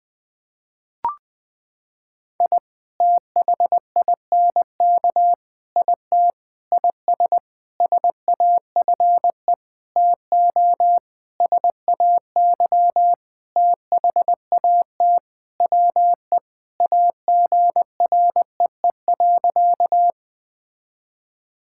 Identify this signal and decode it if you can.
Morse code, heard as a keyed tone, about 20 wpm, 700 Hz. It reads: I THINK IT IS SAFE TO SAY THAT WE AGREE.
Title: Sentences with 8-Word Phrases - 20wpm